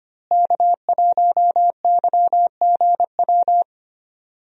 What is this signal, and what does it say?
Morse code, heard as a keyed tone, about 25 wpm, 700 Hz. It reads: K1YGW
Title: USA Call Signs - 5 Characters - Repeated 3x - 25wpm